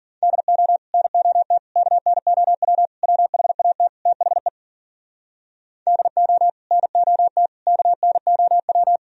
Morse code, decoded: DO NOT KNOW WHAT THE DO NOT KNOW

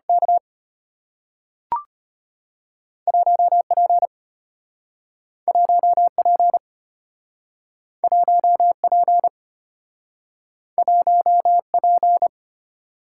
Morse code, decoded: K 1P 1P 1P 1P

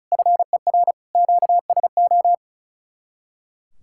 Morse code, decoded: FER QSO

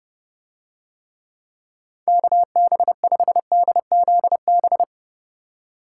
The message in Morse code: KB5DZB

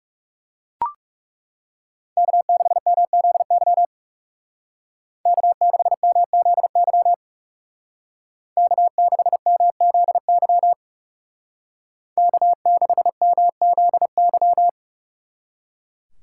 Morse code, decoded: K6MZY K6MZY K6MZY K6MZY

